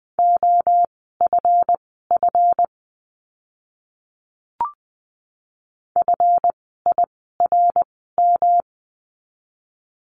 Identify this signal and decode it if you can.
Morse code, heard as a keyed tone, about 20 wpm, spaced as for 14 wpm, 700 Hz. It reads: OFF FIRM
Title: 139 Lesson 23f - Letter f - Words 20wpm